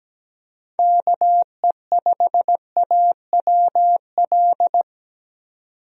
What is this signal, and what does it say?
Morse code, heard as a keyed tone, about 17 wpm, 700 Hz. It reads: KE5AWL
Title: USA Call Signs - 6 Characters - 17wpm